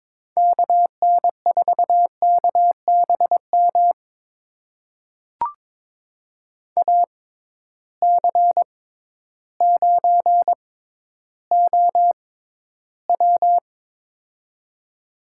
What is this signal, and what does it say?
Morse code, heard as a keyed tone, about 22 wpm, spaced as for 12 wpm, 700 Hz. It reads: KN4KBM AC9OW
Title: USA Call Signs - 6x Character Spacing - 22wpm